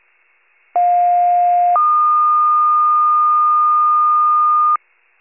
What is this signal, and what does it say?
Two-tone page, heard as a one-second tone, then a three-second tone, about 710 then 1220 Hz. It is heard from 0.7 to 4.8 s.